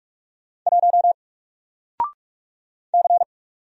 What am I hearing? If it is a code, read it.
Morse code, heard as a keyed tone, about 46 wpm, 700 Hz. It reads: 1 C
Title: Single Character (Letter:Number:Punctuation:Prosign) 45wpm